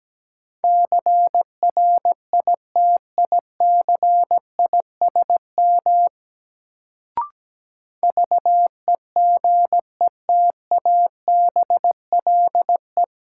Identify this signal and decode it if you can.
Morse code, heard as a keyed tone, about 17 wpm, 700 Hz. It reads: CRITICISM VEGETABLE
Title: Sentences from Top 1300 Words - New Word Drill Alt - 17wpm